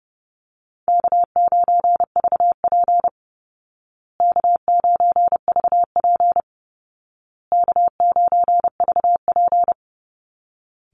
Morse code, heard as a keyed tone, about 30 wpm, 700 Hz. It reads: K9VP K9VP K9VP
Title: Call Signs Repeated 2x - 30wpm